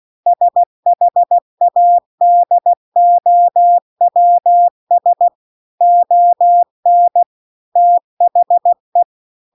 Morse code, heard as a keyed tone, about 16 wpm, 700 Hz. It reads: SHADOWS ON THE